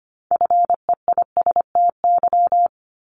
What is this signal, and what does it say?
Morse code, heard as a keyed tone, about 25 wpm, 700 Hz. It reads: FEISTY
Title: Multi-letter Phonemes 25wpm